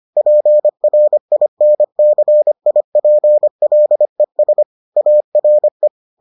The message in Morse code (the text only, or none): PRINCIPLES ARE